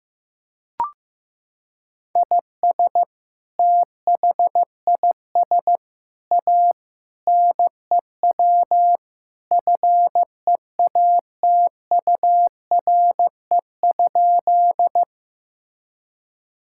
Morse code, heard as a keyed tone, about 15 wpm, 700 Hz. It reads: IS THIS A NEW FEATURE?
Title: Sentences from Top 1000 Words - Review Part 5 - 15wpm